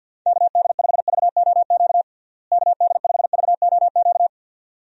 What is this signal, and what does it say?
Morse code, heard as a keyed tone, about 48 wpm, 700 Hz. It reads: KD5VOQ KD5VOQ
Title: USA Call Signs - 6 Characters - Repeated 2x - 50wpm